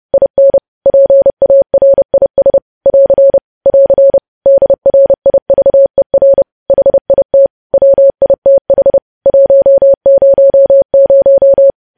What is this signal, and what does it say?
Morse code, heard as a keyed tone, about 30 wpm, 550 Hz. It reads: IN PARIS <AR> <AR> DRIVER HIT WITH 100
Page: news headlines in morse code at 30 wpm